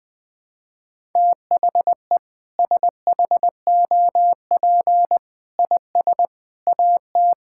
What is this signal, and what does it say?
Morse code, heard as a keyed tone, about 20 wpm, 700 Hz. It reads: THE SHOP IS AT
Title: Sentences with 5-Word Phrases - 20wpm